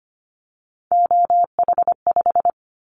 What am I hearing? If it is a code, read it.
Morse code, heard as a keyed tone, about 25 wpm, 700 Hz. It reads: OH5
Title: Callsign Prefixes for Top 25 DXCC - 25wpm